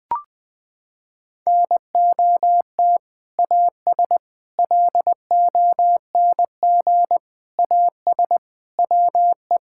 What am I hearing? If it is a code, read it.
Morse code, heard as a keyed tone, about 20 wpm, 700 Hz. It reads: NOT AS LONG AS WE
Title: Sentences from Top 1000 Words - Review Part 6 - 20wpm